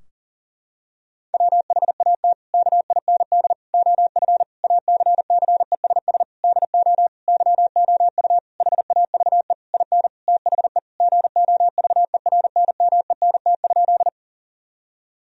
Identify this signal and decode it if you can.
Morse code, heard as a keyed tone, about 40 wpm, 700 Hz. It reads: WHAT KIND OF ACCESS DO YOU HAVE IN THE GOVERNMENT?
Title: Sentences from Top 1000 Words - Review Part 4 - 40wpm